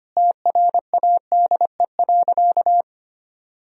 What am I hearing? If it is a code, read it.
Morse code, heard as a keyed tone, about 25 wpm, 700 Hz. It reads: TRADE.
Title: Sentences from Top 1000 Words - Review Part 4 - 25wpm